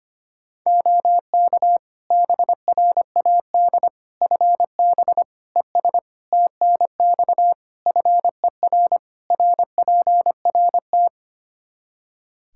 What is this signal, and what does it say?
Morse code, heard as a keyed tone, about 25 wpm, 700 Hz. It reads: OK BRAD FB ES TNX FER RPRT
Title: K1USN SST QSO Practice 25wpm